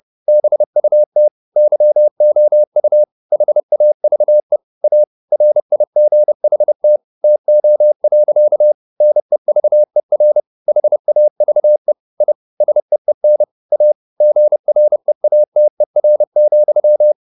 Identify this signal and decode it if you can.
Morse code, heard as a keyed tone, about 30 wpm, 600 Hz. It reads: BUT YOU HAVE A RIGHT TO. NEVER HAVE I SEEN A GREATER,